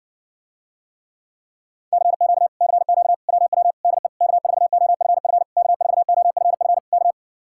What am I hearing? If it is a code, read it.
Morse code, heard as a keyed tone, about 69 wpm, 700 Hz. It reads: CQ CQ WW DE K4OFF K4OFF K